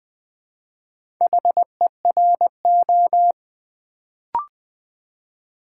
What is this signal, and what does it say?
Morse code, heard as a keyed tone, about 20 wpm, 700 Hz. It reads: HERO
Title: Build A Word - 2 Letters At A Time - 4-Letter Words - 20wpm